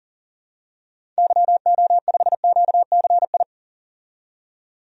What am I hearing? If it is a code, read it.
Morse code, heard as a keyed tone, about 40 wpm, 700 Hz. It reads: YO5QCI
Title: DXCC-100 Call Signs - 6 Characters - 40wpm